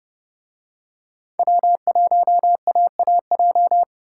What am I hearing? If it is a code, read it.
Morse code, heard as a keyed tone, about 30 wpm, 700 Hz. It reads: W1AAJ